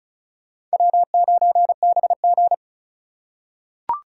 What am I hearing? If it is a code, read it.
Morse code, heard as a keyed tone, about 34 wpm, 700 Hz. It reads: W9BG
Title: USA Call Signs - 4 Characters - 2x Character Spacing - 35wpm